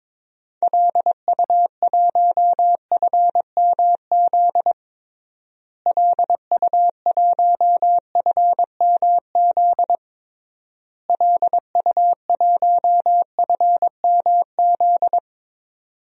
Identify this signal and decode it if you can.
Morse code, heard as a keyed tone, about 22 wpm, 700 Hz. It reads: LU1FMZ LU1FMZ LU1FMZ
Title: DXCC-100 Call Signs - 6 Characters - Repeated 3x - 22wpm